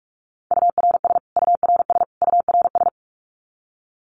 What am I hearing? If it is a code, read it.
Morse code, heard as a keyed tone, about 44 wpm, 700 Hz. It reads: URS URS URS
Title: Abbreviations Repeated 3x - 45wpm